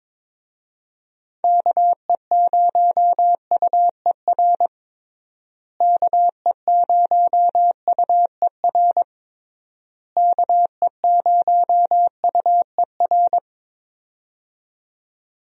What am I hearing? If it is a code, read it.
Morse code, heard as a keyed tone, about 22 wpm, 700 Hz. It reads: KE0UER KE0UER KE0UER
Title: USA Call Signs - 6 Characters - Repeated 3x - 22wpm